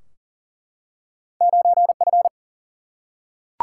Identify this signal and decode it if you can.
Morse code, heard as a keyed tone, about 38 wpm, 700 Hz. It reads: 9F
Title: Single-Letter Single-Number 40wpm